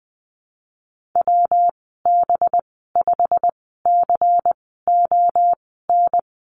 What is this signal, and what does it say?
Morse code, heard as a keyed tone, about 20 wpm, spaced as for 14 wpm, 700 Hz. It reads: WB5CON
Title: 193 Lesson 31g - Letter v - Call Signs 20wpm